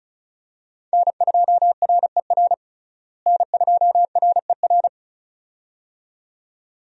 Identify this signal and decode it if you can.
Morse code, heard as a keyed tone, about 34 wpm, 700 Hz. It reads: N2RER N2RER
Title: USA Call Signs - 5 Characters - Repeated 2x - 35wpm